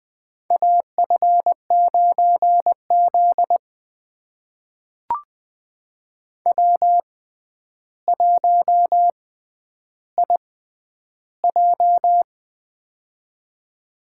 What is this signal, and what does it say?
Morse code, heard as a keyed tone, about 20 wpm, spaced as for 10 wpm, 700 Hz. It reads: AF9Z W1IJ
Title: USA Call Signs - 4 Characters - 6x Character Spacing - 20wpm